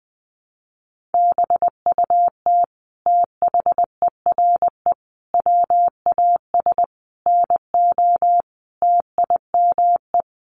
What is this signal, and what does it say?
Morse code, heard as a keyed tone, about 20 wpm, 700 Hz. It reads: BUT THERE WAS NO TIME
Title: Sentences from Top 1400 Words 20wpm